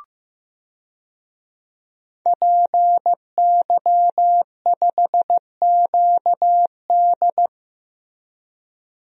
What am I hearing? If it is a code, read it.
Morse code, heard as a keyed tone, about 15 wpm, 700 Hz. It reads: PY5QD
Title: DXCC-100 Call Signs - 5 Characters - Repeated 3x - 15wpm